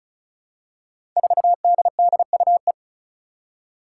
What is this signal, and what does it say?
Morse code, heard as a keyed tone, about 36 wpm, 700 Hz. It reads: 4DDUE
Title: Five Challenging Letter-Number Combinations 35wpm